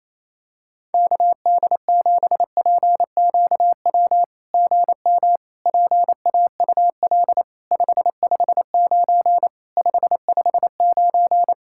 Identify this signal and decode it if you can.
Morse code, heard as a keyed tone, about 28 wpm, 700 Hz. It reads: KD7PQW GM PAUL 559 559
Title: SOTA QSO Practice 28wpm